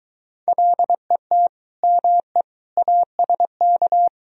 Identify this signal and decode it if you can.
Morse code, heard as a keyed tone, about 23 wpm, 700 Hz. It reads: LET ME ASK